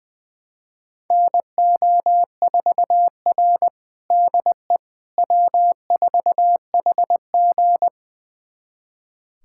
Morse code, heard as a keyed tone, about 21 wpm, 700 Hz. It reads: NO4R DE W4HG
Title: USA Callsign DE Callsign - 4 Characters - 20wpm